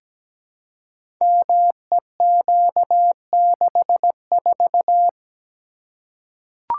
Morse code, heard as a keyed tone, about 17 wpm, 700 Hz. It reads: MEQ64
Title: Code Groups - 5 Letter:Numbers - 17wpm